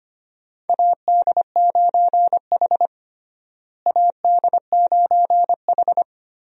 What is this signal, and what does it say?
Morse code, heard as a keyed tone, about 25 wpm, 700 Hz. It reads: AD9H AD9H
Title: USA Call Signs - 4 Characters - Repeated 3x - 25wpm